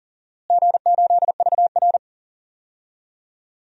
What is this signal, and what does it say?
Morse code, heard as a keyed tone, about 40 wpm, 700 Hz. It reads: G8VR
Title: DXCC-100 Call Signs - 4 Characters - Repeated 2x - 40wpm